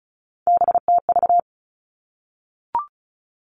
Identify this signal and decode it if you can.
Morse code, heard as a keyed tone, about 36 wpm, 700 Hz. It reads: BTV
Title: Three Challenging Letter-Number Combinations 35wpm